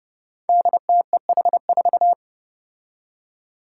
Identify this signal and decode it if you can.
Morse code, heard as a keyed tone, about 32 wpm, 700 Hz. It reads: DTEH4
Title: Five Challenging Letter-Number Combinations 30wpm